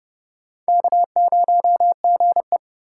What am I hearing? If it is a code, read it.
Morse code, heard as a keyed tone, about 29 wpm, 700 Hz. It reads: K0GE